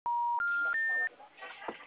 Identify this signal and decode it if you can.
Special information tone, with speech.